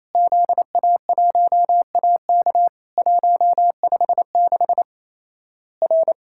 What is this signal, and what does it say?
Morse code, heard as a keyed tone, about 28 wpm, 700 Hz. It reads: ZA1AK 156 R
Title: Worked All Europe DX Contest - QTC Exchange - 28wpm